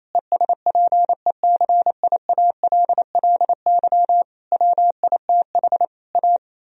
Morse code, heard as a keyed tone, about 28 wpm, 700 Hz. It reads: ESPECIALLY WITH A
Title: Sentences from Top 1000 Words - Review Part 4 - 28wpm